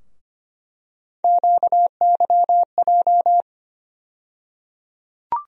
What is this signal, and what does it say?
Morse code, heard as a keyed tone, about 25 wpm, 700 Hz. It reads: QYJ
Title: USA Call Sign Postfixes - 3 Letters 25wpm